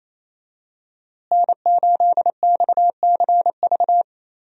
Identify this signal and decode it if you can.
Morse code, heard as a keyed tone, about 28 wpm, 700 Hz. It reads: N8XCV
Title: USA Call Signs - 5 Characters - Repeated 3x - 28wpm